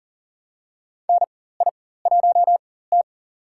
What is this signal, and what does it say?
Morse code, heard as a keyed tone, about 40 wpm, spaced as for 19 wpm, 700 Hz. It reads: NI1T
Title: USA Call Signs - 4x Character Spacing - 40wpm